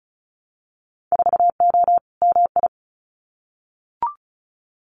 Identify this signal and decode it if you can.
Morse code, heard as a keyed tone, about 36 wpm, 700 Hz. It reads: 4O MI
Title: Winter Field Day Exchanges Encore 35wpm